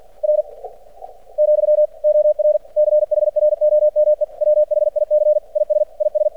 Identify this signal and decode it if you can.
Morse code, heard as a keyed tone, about 50 wpm, 600 Hz. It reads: R , OR CHROME WHIC IS IS
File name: archived / CRSnet